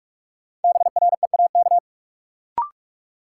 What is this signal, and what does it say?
Morse code, heard as a keyed tone, about 46 wpm, 700 Hz. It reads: BREAK
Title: Sentences from Top 900 Words - New Word Drill - 45wpm